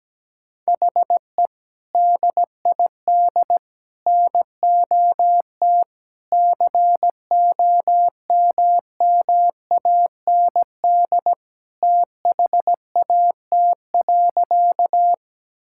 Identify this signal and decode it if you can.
Morse code, heard as a keyed tone, about 17 wpm, 700 Hz. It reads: HE DID NOT COMMAND THAT.
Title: Sentences from Top 1600 Words 17wpm